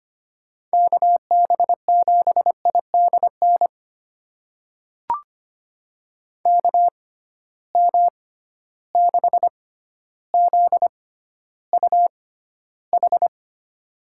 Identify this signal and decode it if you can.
Morse code, heard as a keyed tone, about 25 wpm, spaced as for 13 wpm, 700 Hz. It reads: KB7IDN KM6ZUH
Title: USA Call Signs - 6 Characters - 6x Character Spacing - 25wpm